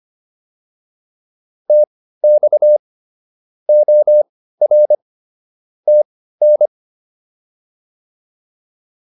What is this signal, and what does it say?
Morse code, heard as a keyed tone, about 25 wpm, spaced as for 15 wpm, 600 Hz. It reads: TX OR TN